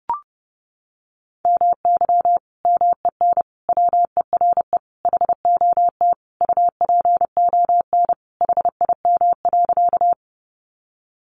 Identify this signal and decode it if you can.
Morse code, heard as a keyed tone, about 30 wpm, 700 Hz. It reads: MY MEN WERE HOT UPON HIM.
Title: Sentences from Top 1100 Words 30wpm